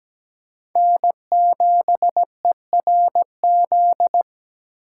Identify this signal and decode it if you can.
Morse code, heard as a keyed tone, about 17 wpm, 700 Hz. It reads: N7ERZ
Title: USA Call Signs - 5 Characters - 8x Character Spacing - 17wpm